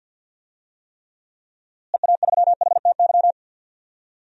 Morse code, heard as a keyed tone, about 51 wpm, 700 Hz. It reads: EA3HTY
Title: DXCC-100 Call Signs - 6 Characters - Repeated 2x - 50wpm